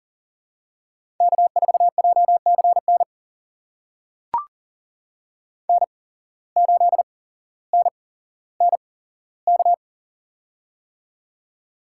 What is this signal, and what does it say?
Morse code, heard as a keyed tone, about 40 wpm, spaced as for 17 wpm, 700 Hz. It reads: K4JCN N8NNK
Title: USA Call Signs - 8x Character Spacing - 40wpm